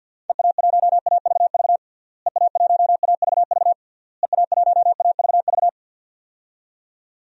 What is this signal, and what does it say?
Morse code, heard as a keyed tone, about 51 wpm, 700 Hz. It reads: EA1AVV EA1AVV EA1AVV
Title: DXCC-100 Call Signs - 6 Characters - Repeated 3x - 50wpm